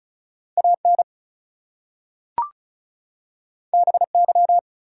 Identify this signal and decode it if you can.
Morse code, heard as a keyed tone, about 36 wpm, 700 Hz. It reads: AN BY
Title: Two Letter Words - 35wpm